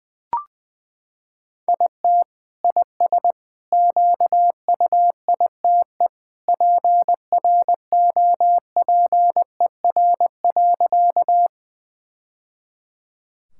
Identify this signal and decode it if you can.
Morse code, heard as a keyed tone, about 21 wpm, 700 Hz. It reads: IT IS QUITE PROPER.
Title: Sentences from Top 1000 Words - Review Part 5 - 20wpm